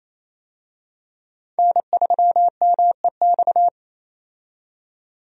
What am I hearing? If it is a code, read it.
Morse code, heard as a keyed tone, about 28 wpm, 700 Hz. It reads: N3MEX
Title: USA Call Signs - 5 Characters - 28wpm